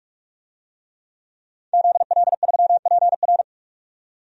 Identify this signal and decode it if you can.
Morse code, heard as a keyed tone, about 42 wpm, 700 Hz. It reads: ZL3PR
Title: DXCC-100 Call Signs - 5 Characters - 45wpm